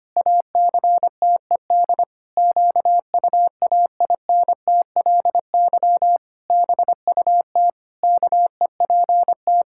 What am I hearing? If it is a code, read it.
Morse code, heard as a keyed tone, about 25 wpm, 700 Hz. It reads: ACTED QUAINTLY BUT KEPT
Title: Pangram Challenge v2 - 25wpm